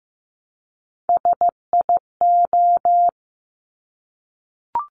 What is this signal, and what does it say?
Morse code, heard as a keyed tone, about 15 wpm, 700 Hz. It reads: SIO
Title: Top Three Letter Combinations in Words 15wpm